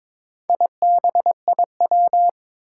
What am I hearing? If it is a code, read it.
Morse code, heard as a keyed tone, about 21 wpm, 700 Hz. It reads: IBIW
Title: Four Challenging Letter-Number Combinations 22wpm